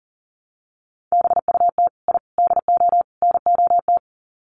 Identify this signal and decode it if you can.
Morse code, heard as a keyed tone, about 39 wpm, 700 Hz. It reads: BUT I DO NOT
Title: Sets of 4 Words 40wpm